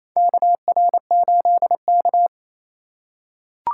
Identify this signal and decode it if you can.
Morse code, heard as a keyed tone, about 28 wpm, 700 Hz. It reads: KR8K
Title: USA Call Signs - 4 Characters - 3x Character Spacing - 28wpm